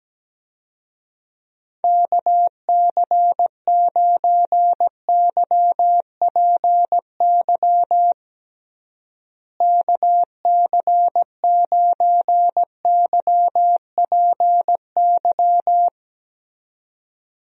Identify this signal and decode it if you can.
Morse code, heard as a keyed tone, about 17 wpm, 700 Hz. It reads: KC9YPY KC9YPY
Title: USA Call Signs - 6 Characters - Repeated 2x - 17wpm